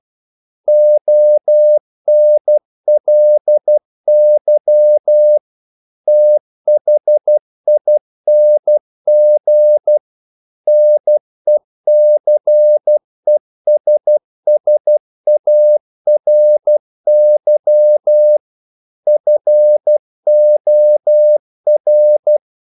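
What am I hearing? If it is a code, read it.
Morse code, heard as a keyed tone, about 12 wpm, 600 Hz. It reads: ONLY THING NECESSARY FOR
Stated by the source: Morse practice sending English quotations.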